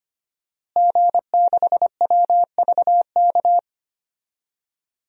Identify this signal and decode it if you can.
Morse code, heard as a keyed tone, about 25 wpm, 700 Hz. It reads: G6WVK
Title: DXCC-100 Call Signs - 5 Characters - Repeated 3x - 25wpm